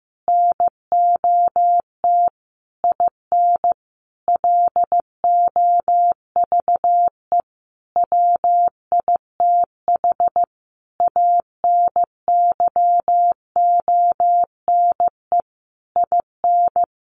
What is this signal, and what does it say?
Morse code, heard as a keyed tone, about 15 wpm, 700 Hz. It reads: NOT IN LOVE WITH ANYONE IN
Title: Sentences from Top 900 Words 15wpm